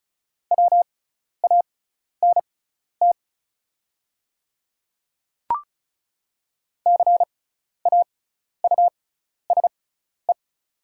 Morse code, heard as a keyed tone, about 36 wpm, spaced as for 12 wpm, 690 Hz. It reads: WANT CAUSE
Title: 6x Character Spacing - Top 200 Words - 35wpm